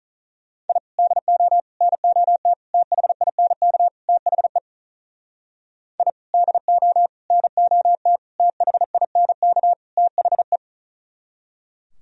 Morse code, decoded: I DO NOT THINK THE I DO NOT THINK THE

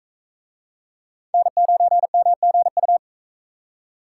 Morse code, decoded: N9MGU

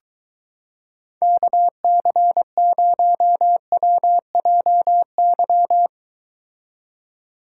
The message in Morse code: KC0WJY

